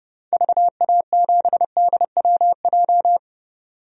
Morse code, heard as a keyed tone, about 30 wpm, 700 Hz. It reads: VA7DWJ
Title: DXCC-100 Call Signs - 6 Characters - 30wpm